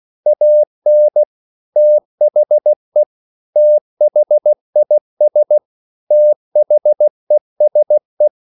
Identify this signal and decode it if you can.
Morse code, heard as a keyed tone, about 16 wpm, 600 Hz. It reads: AN THE THIS THESE
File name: CW / Practice / 500MostCommonEnglishWords16